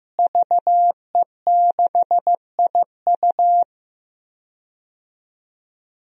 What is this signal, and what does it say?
Morse code, heard as a keyed tone, about 16 wpm, 700 Hz. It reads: VE6IU
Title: DXCC-100 Call Signs - 5 Characters - Repeated 3x - 15wpm